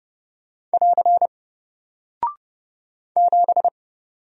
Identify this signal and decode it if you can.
Morse code, heard as a keyed tone, about 30 wpm, 700 Hz. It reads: <AR> 7
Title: Single Character (Letter:Number:Punctuation:Prosign) Alt 30wpm